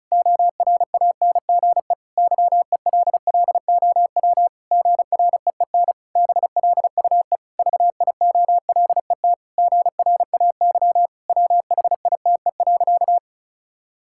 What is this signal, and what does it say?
Morse code, heard as a keyed tone, about 34 wpm, 700 Hz. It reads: ORANGE YELLOW GREEN BLUE VIOLET GRAY WHITE.
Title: Pangram Challenge v2 - 35wpm